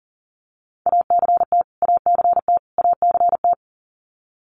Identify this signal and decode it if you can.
Morse code, heard as a keyed tone, about 40 wpm, 700 Hz. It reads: ACT ACT ACT
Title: Top 500 Words Repeated 40wpm